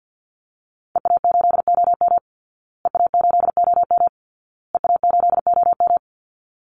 Morse code, heard as a keyed tone, about 50 wpm, 700 Hz. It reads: EA8OM EA8OM EA8OM
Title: Call Signs Repeated 2x - 50wpm